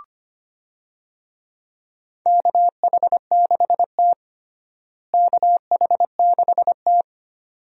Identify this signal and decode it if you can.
Morse code, heard as a keyed tone, about 25 wpm, 700 Hz. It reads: KH6T KH6T
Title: DXCC-100 Call Signs - 4 Characters - Repeated 2x - 25wpm